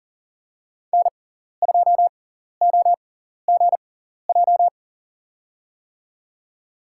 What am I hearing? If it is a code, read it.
Morse code, heard as a keyed tone, about 39 wpm, spaced as for 14 wpm, 700 Hz. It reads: N2OGJ
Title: USA Call Signs - 5 Characters - 6x Character Spacing - 40wpm